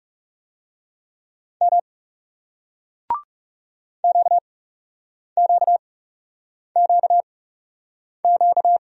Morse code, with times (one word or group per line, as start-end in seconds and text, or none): 1.6-1.8 M
4.0-8.8 QQQQ